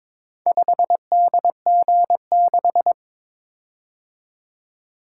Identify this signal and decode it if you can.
Morse code, heard as a keyed tone, about 22 wpm, 700 Hz. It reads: 5DG6
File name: practice / 22wpm / Four Letter-Number Combinations - 22wpm